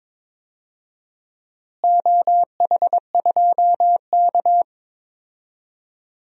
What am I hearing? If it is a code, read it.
Morse code, heard as a keyed tone, about 22 wpm, 700 Hz. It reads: OH2K
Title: DXCC-100 Call Signs - 4 Characters - Repeated 2x - 22wpm